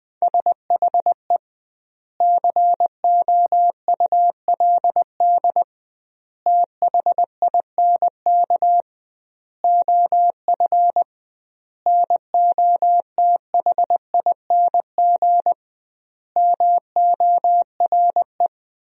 Morse code, heard as a keyed tone, about 20 wpm, 700 Hz. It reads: SHE COULD THINK OF NOTHING MORE
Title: 2x Word-Spacing - Sentences from Top 500 Words - 20wpm